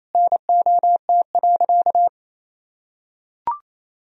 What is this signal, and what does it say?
Morse code, heard as a keyed tone, about 28 wpm, 700 Hz. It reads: NOT.
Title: Sentences from Top 1000 Words - Review Part 3 - 28wpm